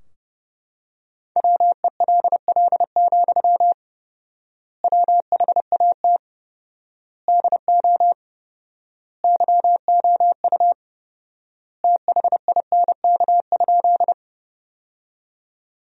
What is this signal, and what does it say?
Morse code, heard as a keyed tone, about 30 wpm, 700 Hz. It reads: WELL, WHAT DO YOU THINK?
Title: 4x Word-Spacing - Sentences from Top 200 Words - 30wpm